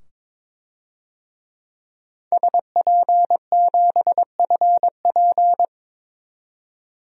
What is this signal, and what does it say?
Morse code, heard as a keyed tone, about 21 wpm, 700 Hz. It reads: SP7FP
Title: DXCC-100 Call Signs - 5 Characters - 22wpm